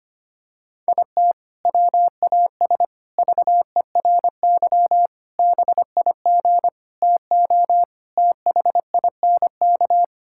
Morse code, heard as a keyed tone, about 25 wpm, 700 Hz. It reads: IT WAS VERY BIG TO THINK
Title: Sentences from Top 1600 Words 25wpm